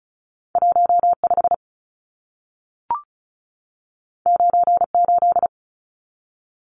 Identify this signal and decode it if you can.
Morse code, heard as a keyed tone, about 36 wpm, 700 Hz. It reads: 15 98
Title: Numbers - 2-Digit 35wpm